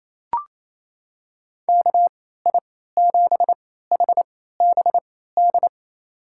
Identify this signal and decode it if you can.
Morse code, heard as a keyed tone, about 27 wpm, spaced as for 16 wpm, 700 Hz. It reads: KI7HBD
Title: USA Call Signs - 6 Characters - 3x Character Spacing - 28wpm